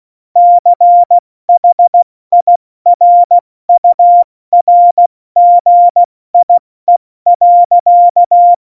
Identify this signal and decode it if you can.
Morse code, heard as a keyed tone, about 16 wpm, spaced as for 14 wpm, 700 Hz. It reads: CHIRURGIE.
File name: Morsekurs / Witze / Witz 01. Dezember 2020, 70 F80